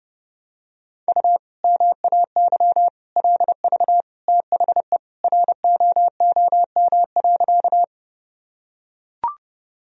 Morse code, heard as a keyed tone, about 32 wpm, 700 Hz. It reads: U MAY LV THE ROOM.
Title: Sentences from Top 400 Words and Abbreviations 30wpm